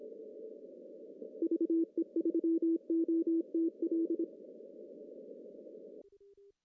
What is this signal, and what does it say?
Morse code, heard as a keyed tone, about 26 wpm, 330 Hz. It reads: VE3OTL
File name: CW / cqww-cw-k5tr